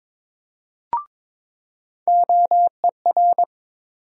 Morse code, heard as a keyed tone, about 22 wpm, 700 Hz. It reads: OER
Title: USA Call Sign Postfixes - 3 Letters 22wpm